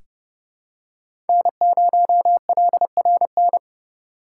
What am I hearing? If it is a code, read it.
Morse code, heard as a keyed tone, about 32 wpm, 700 Hz. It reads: N0LRN
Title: USA Call Signs - 6x Character Spacing - 30wpm